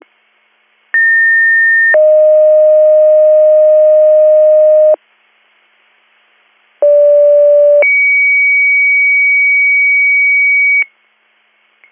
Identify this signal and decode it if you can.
Two-tone page, heard as a one-second tone, then a three-second tone, about 1800 then 620 Hz, about 580 then 2250 Hz. It is heard from 0.9 to 10.8 s.